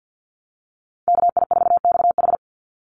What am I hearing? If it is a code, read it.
Morse code, heard as a keyed tone, about 48 wpm, 700 Hz. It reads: KI4XH
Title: Call Signs Repeated 3x - 50wpm